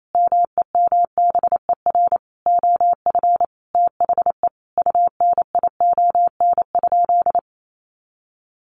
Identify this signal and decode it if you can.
Morse code, heard as a keyed tone, about 28 wpm, 700 Hz. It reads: MEMBER OF THE UNION?
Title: Sentences from Top 1400 Words 28wpm